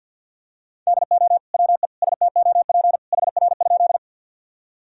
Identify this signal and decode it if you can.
Morse code, heard as a keyed tone, about 52 wpm, 700 Hz. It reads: DO WE STOP HR?
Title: Sentences from Top 300 Words and Abbreviations 50wpm